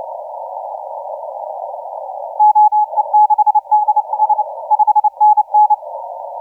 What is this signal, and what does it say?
Morse code, heard as a keyed tone, about 28 wpm, 850 Hz, with radio noise. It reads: OE6DS 5NN
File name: A35UD 0956UTC 17m